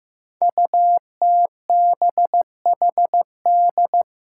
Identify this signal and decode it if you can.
Morse code, heard as a keyed tone, about 15 wpm, 700 Hz. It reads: UTBHD
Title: Five Challenging Letter-Number Combinations 15wpm